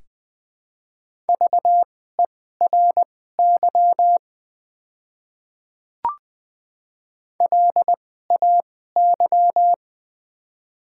Morse code, heard as a keyed tone, about 20 wpm, spaced as for 14 wpm, 700 Hz. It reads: VERY LAY
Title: 2x Character Spacing - Top 500 Words - 20wpm